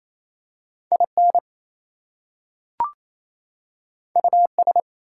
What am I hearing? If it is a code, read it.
Morse code, heard as a keyed tone, about 28 wpm, 700 Hz. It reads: IN US
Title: Two Letter Words - 28wpm